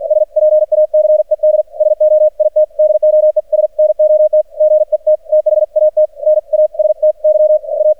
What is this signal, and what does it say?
Morse code, heard as a keyed tone, about 48 wpm, 610 Hz. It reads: SPACED SO IT DOES NOT GET THAT NASTY A
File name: archived / CRSnet